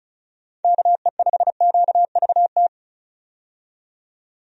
Morse code, heard as a keyed tone, about 35 wpm, 700 Hz. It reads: KE5QVT
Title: USA Call Signs - 6 Characters - 35wpm